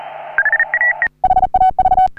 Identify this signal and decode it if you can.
Morse code, heard as a keyed tone, about 40 wpm, 750 Hz. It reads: HAV
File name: CW / HA3OU RHOBCW REC